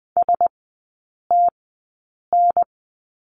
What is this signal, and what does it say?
Morse code, heard as a keyed tone, about 20 wpm, spaced as for 8 wpm, 700 Hz. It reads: STN
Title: 035 Lesson 7e - Letter s - Three Characters 20wpm